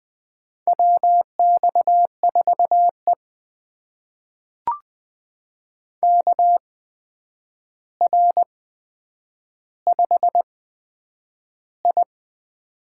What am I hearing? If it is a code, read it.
Morse code, heard as a keyed tone, about 20 wpm, spaced as for 9 wpm, 700 Hz. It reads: WX4E KR5I
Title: USA Call Signs - 4 Characters - 8x Character Spacing - 20wpm